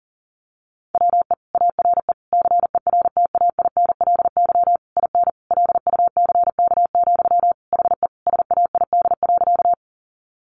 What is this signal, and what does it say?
Morse code, heard as a keyed tone, about 40 wpm, 700 Hz. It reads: WE ARE CERTAINLY IN LUCK, HE SAID.
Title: Sentences from Top 1500 Words 40wpm